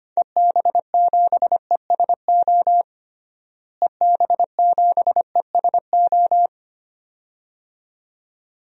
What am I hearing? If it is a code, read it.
Morse code, heard as a keyed tone, about 25 wpm, 700 Hz. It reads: EB7ESO EB7ESO
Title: DXCC-100 Call Signs - 6 Characters - Repeated 2x - 25wpm